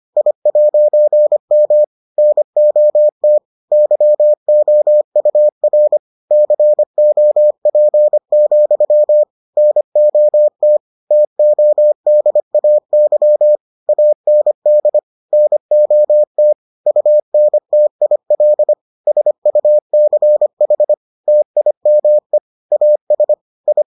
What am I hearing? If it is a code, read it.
Morse code, heard as a keyed tone, about 25 wpm, 600 Hz. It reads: I'M NOT YOUR COP, NOT TODAY AND NOT UNTIL SUCH TIME AS I